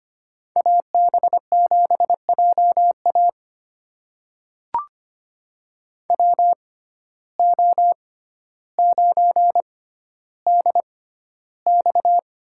Morse code, heard as a keyed tone, about 25 wpm, spaced as for 13 wpm, 700 Hz. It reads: AB7JA WO9DX
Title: USA Call Signs - 5 Characters - 6x Character Spacing - 25wpm